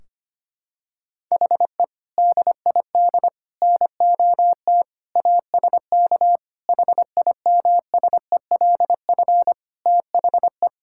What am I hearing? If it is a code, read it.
Morse code, heard as a keyed tone, about 25 wpm, 700 Hz. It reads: HE DID NOT ASK HIMSELF THE
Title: Sentences from Top 1600 Words 25wpm